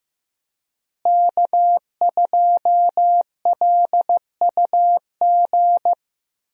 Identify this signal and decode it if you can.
Morse code, heard as a keyed tone, about 15 wpm, 700 Hz. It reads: K2LUG
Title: USA Call Signs - 5 Characters - Repeated 3x - 15wpm